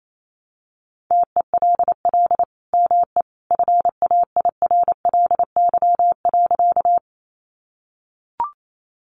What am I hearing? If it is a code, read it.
Morse code, heard as a keyed tone, about 28 wpm, 700 Hz. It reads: TELL ME FAIRLY.
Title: Sentences from Top 1100 Words 28wpm